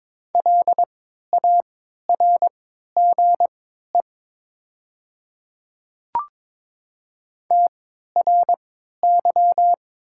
Morse code, heard as a keyed tone, about 22 wpm, spaced as for 13 wpm, 700 Hz. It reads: LARGE TRY
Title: 3x Character Spacing - Top 200 Words - 22wpm